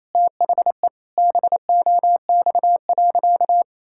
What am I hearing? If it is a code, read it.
Morse code, heard as a keyed tone, about 28 wpm, 700 Hz. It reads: THE BOX.
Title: Sentences from Top 1000 Words - Review Part 6 - 28wpm